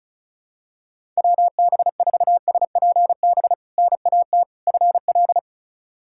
Morse code, heard as a keyed tone, about 34 wpm, 700 Hz. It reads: WB4SPB NAT FL
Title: K1USN SST QSO Practice - 35wpm